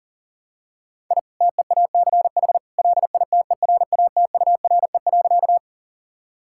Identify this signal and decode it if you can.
Morse code, heard as a keyed tone, about 40 wpm, 700 Hz. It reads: I TEACH LITERATURE.
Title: Sentences from Top 1000 Words - Review Part 6 - 40wpm